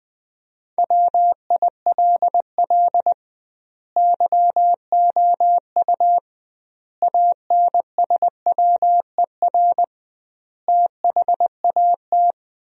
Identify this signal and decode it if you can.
Morse code, heard as a keyed tone, about 19 wpm, 700 Hz. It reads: WILL YOU ANSWER THAT
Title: 2x Word-Spacing - Sentences from Top 400 Words - 20wpm